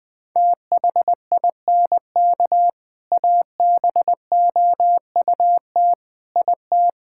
Morse code, heard as a keyed tone, about 20 wpm, 700 Hz. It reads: THINK ABOUT IT
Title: Sentences with 3-Word Phrases - 20wpm